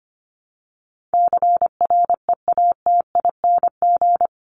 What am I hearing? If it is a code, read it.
Morse code, heard as a keyed tone, about 25 wpm, 700 Hz. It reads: CREATING
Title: ING Suffix - Top 100 Words with Base Words 25wpm